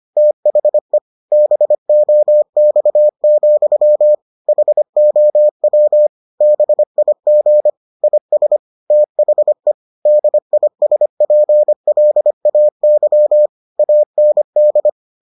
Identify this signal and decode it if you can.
Morse code, heard as a keyed tone, about 25 wpm, 600 Hz. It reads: THE BOX, HOW BIG IS THE DISPLAY AND